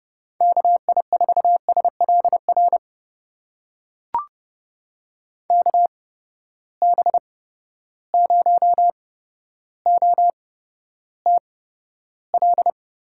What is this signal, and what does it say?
Morse code, heard as a keyed tone, about 31 wpm, spaced as for 13 wpm, 700 Hz. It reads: KI4SLR KB0OTL
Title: USA Call Signs - 6 Characters - 8x Character Spacing - 30wpm